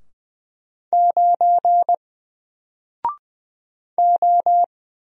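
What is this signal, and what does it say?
Morse code, heard as a keyed tone, about 20 wpm, 700 Hz. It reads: 9 O
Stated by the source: Single letter and number Morse drill at 20 wpm, from N4LCD.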